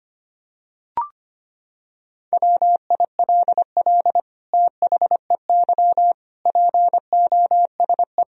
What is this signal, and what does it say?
Morse code, heard as a keyed tone, about 25 wpm, 700 Hz. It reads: WILL THEY POSE
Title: Sentences from Top 1000 Words - Review Part 2 - 25wpm